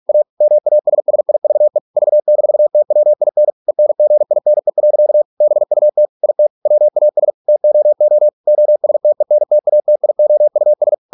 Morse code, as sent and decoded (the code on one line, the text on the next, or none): .- / -- .- ... ... .. ...- . / ...- -....- - .-- .. -. / . -. --. .. -. .-.-.- / -... ..- - / .. - / .-- .- ... / - --- --- / --- ... - . -. - .- - .. --- ..- ...
A MASSIVE V-TWIN ENGINE. BUT IT WAS TOO OSTENTATIOUS